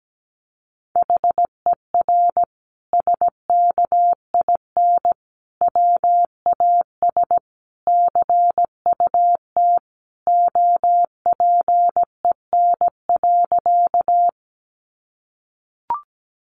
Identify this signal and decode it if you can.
Morse code, heard as a keyed tone, about 17 wpm, 700 Hz. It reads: HER SKIN WAS CUT OPEN.
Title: Sentences from Top 1300 Words 17wpm